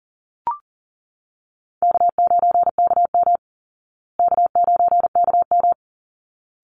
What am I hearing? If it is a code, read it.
Morse code, heard as a keyed tone, about 40 wpm, 700 Hz. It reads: K9KM K9KM
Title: Easy Call Signs Repeated 2x - 40wpm